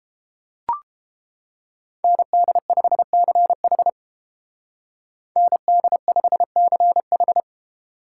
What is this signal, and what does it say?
Morse code, heard as a keyed tone, about 30 wpm, 700 Hz. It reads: ND5CH ND5CH